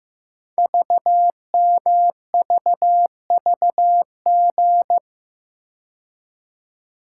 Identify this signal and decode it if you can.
Morse code, heard as a keyed tone, about 15 wpm, 700 Hz. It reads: VMVVG
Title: Code Groups - 5 Letters - 15wpm